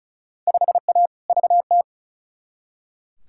Morse code, heard as a keyed tone, about 36 wpm, 700 Hz. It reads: 5A VT
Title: Field Day Exchanges Encore - 35wpm